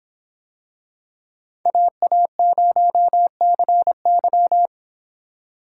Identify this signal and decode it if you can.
Morse code, heard as a keyed tone, about 25 wpm, 700 Hz. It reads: AA0CY